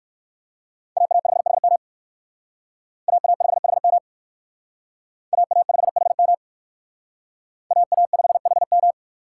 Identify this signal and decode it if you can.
Morse code, heard as a keyed tone, about 56 wpm, 700 Hz. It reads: AA5HM AA5HM AA5HM AA5HM